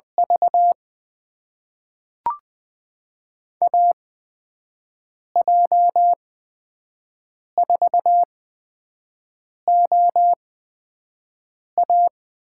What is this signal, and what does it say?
Morse code, heard as a keyed tone, about 20 wpm, spaced as for 5 wpm, 700 Hz. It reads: V AJ4OA
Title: USA Call Signs - 5 Characters - 8x Character Spacing - 20wpm